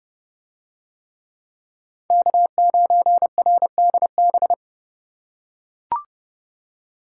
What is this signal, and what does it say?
Morse code, heard as a keyed tone, about 31 wpm, 700 Hz. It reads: K9RDB